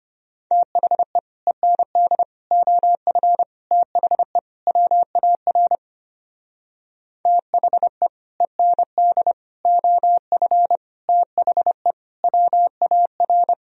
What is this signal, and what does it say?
Morse code, heard as a keyed tone, about 30 wpm, 700 Hz. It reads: THE END OF THE WAR THE END OF THE WAR
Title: Speed-Racing - Sets of 5 Words 25wpm